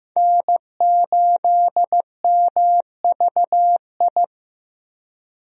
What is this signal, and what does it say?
Morse code, heard as a keyed tone, about 15 wpm, 700 Hz. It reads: N8MVI
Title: USA Call Signs - 5 Characters - Repeated 2x - 15wpm